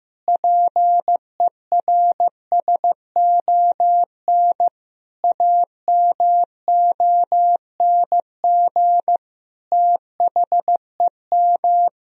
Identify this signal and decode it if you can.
Morse code, heard as a keyed tone, about 16 wpm, 700 Hz. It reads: PERSON AMONG THEM
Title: Sentences from Top 1000 Words - Review Part 3 - 15wpm